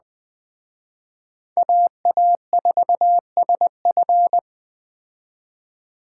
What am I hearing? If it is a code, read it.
Morse code, heard as a keyed tone, about 20 wpm, 700 Hz. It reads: AA4SF
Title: Speed Racing - USA 2x2 Call Signs 17wpm